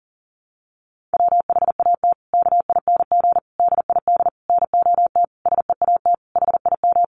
Morse code, heard as a keyed tone, about 40 wpm, 700 Hz. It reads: WHAT KING DID NOT SEAT HIM
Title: Sentences from Top 1500 Words 40wpm